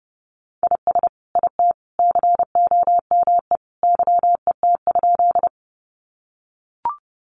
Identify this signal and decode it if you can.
Morse code, heard as a keyed tone, about 30 wpm, 700 Hz. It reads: IS IT COME YET?
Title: Sentences from Top 500 Words 30wpm